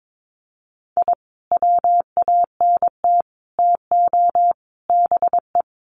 Morse code, heard as a keyed tone, about 22 wpm, 700 Hz. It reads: I WANT TO BE
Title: Sets of 4 Words 22wpm